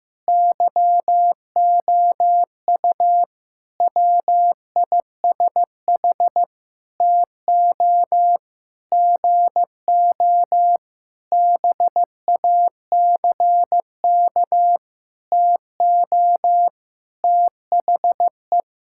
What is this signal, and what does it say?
Morse code, heard as a keyed tone, about 15 wpm, 700 Hz. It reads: YOU WISH TO GO BACK TO THE
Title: Sentences from Top 1600 Words 15wpm